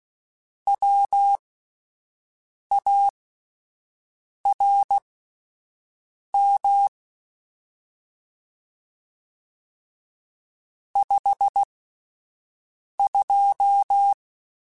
Morse code, heard as a keyed tone, about 16 wpm, spaced as for 5 wpm, 800 Hz. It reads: WARM 52
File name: Spaced Characters / Spaced Characters-mp3s / Weather 4 10 sp @10